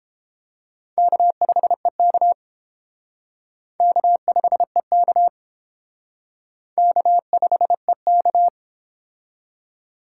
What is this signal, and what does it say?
Morse code, heard as a keyed tone, about 30 wpm, 700 Hz. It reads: K5EK K5EK K5EK